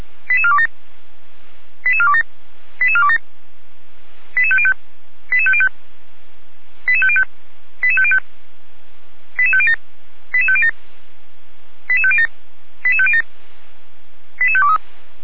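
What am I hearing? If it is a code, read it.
Five-tone selective call (ZVEI), heard as quick runs of five tones, tones of about 70 ms. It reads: 88427 88427 88427 88505 88505 88505 88505 88507 88507 88507 88507 88523
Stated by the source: ZVEI five-tone selective call received by SDR.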